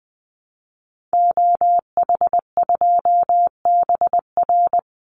Morse code, heard as a keyed tone, about 20 wpm, 700 Hz. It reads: OH2BR
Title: Call Signs Repeated 3x - 20wpm